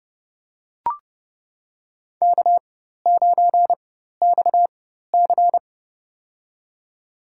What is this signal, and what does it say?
Morse code, heard as a keyed tone, about 32 wpm, spaced as for 14 wpm, 700 Hz. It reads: K9XC
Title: USA Call Signs - 4x Character Spacing - 30wpm